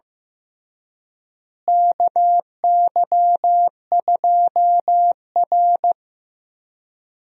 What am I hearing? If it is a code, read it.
Morse code, heard as a keyed tone, about 15 wpm, 700 Hz. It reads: KY2R
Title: USA Call Signs - 4 Characters - Repeated 3x - 15wpm